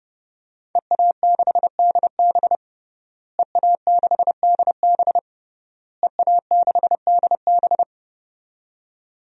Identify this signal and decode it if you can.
Morse code, heard as a keyed tone, about 30 wpm, 700 Hz. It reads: EA6DB EA6DB EA6DB
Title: DXCC-100 Call Signs - 5 Characters - Repeated 3x - 30wpm